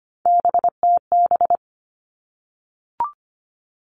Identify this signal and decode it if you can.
Morse code, heard as a keyed tone, about 25 wpm, 700 Hz. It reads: BTB
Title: All Cut Numbers - 3-Digit 25wpm